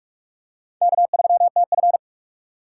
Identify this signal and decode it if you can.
Morse code, heard as a keyed tone, about 45 wpm, 700 Hz. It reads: K3TF
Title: K1USN SST QSO Practice - 45wpm